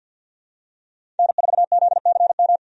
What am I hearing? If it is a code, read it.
Morse code, heard as a keyed tone, about 51 wpm, 700 Hz. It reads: N4ZCM
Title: USA Call Signs - 5 Characters - 50wpm